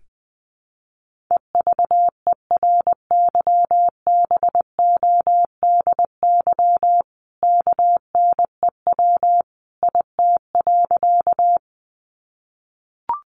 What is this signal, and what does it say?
Morse code, heard as a keyed tone, about 20 wpm, 700 Hz. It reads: EVERYBODY KNEW IT.
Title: Sentences from Top 1400 Words 20wpm